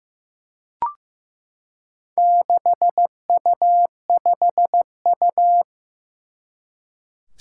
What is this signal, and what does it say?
Morse code, heard as a keyed tone, about 15 wpm, 700 Hz. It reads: 6U5U
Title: Four Challenging Letter-Number Combinations 15wpm